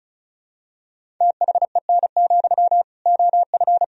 Morse code, heard as a keyed tone, about 34 wpm, 700 Hz. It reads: THEN, OF
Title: Sentences from Top 1000 Words - Review Part 1 - 35wpm